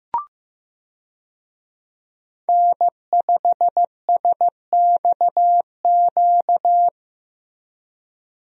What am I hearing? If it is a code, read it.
Morse code, heard as a keyed tone, about 15 wpm, 700 Hz. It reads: N5SXQ
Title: USA Call Signs - 5 Characters - Repeated 2x - 15wpm